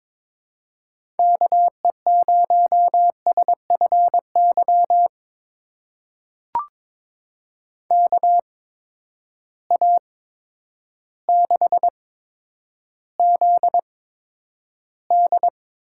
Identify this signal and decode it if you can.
Morse code, heard as a keyed tone, about 22 wpm, spaced as for 10 wpm, 700 Hz. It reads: KE0SFY KA6ZD
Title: USA Call Signs - 6 Characters - 8x Character Spacing - 22wpm